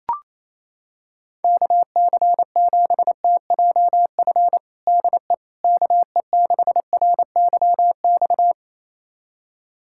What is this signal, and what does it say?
Morse code, heard as a keyed tone, about 28 wpm, 700 Hz. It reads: KC7TJF DE KE6RYX
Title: USA Callsign DE Callsign - 6 Characters - 28wpm